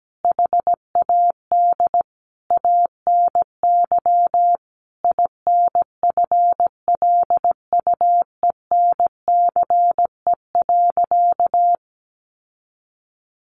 Morse code, heard as a keyed tone, about 17 wpm, 700 Hz. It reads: HAD ANY INFLUENCE.
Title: Sentences from Top 1100 Words 17wpm